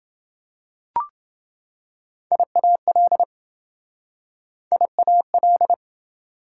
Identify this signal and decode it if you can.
Morse code, heard as a keyed tone, about 28 wpm, 700 Hz. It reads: IAL IAL